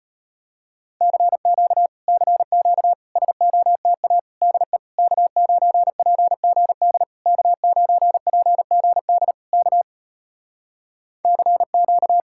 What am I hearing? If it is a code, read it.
Morse code, heard as a keyed tone, about 38 wpm, 700 Hz. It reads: CQ CQ SOTA DE K9PGD K9PGD K CQ